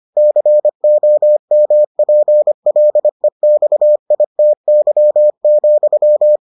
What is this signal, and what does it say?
Morse code, heard as a keyed tone, about 25 wpm, 600 Hz. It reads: COMPLEXITY,